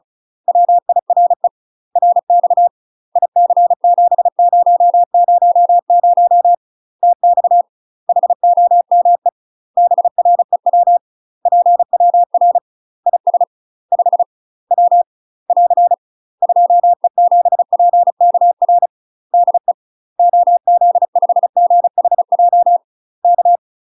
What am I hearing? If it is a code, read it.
Morse code, heard as a keyed tone, about 35 wpm, 700 Hz. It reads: WIRE RX IC7000 TX HOME BREW PWR IS 5 W <AR> 2E7PKR DE OZ5GHJ K
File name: UK MASTER QSOs TRAINING / 35WPM / UK MASTER QSOs 0004